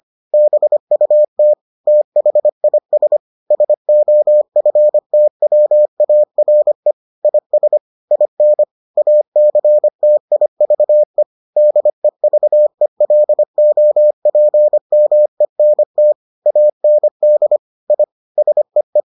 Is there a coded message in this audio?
Yes: Morse code from 0.3 to 19.0 s, heard as a keyed tone, about 25 wpm, 600 Hz. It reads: BUT THIS SOFTWARE IS IN ACTIVE DEVELOPMENT AND I SEE